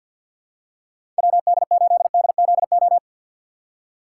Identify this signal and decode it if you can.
Morse code, heard as a keyed tone, about 46 wpm, 690 Hz. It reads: WD8DZO
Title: USA Call Signs - 2x Character Spacing - 50wpm